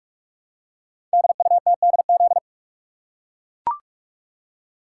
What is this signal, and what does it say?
Morse code, heard as a keyed tone, about 44 wpm, 690 Hz. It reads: DUTDZ
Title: Code Groups - 5 Letters - 45wpm